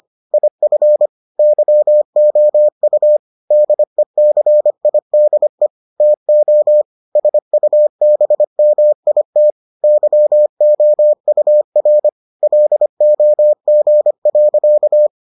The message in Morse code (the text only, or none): IF YOU DECIDE TO SUBMIT YOUR LOG.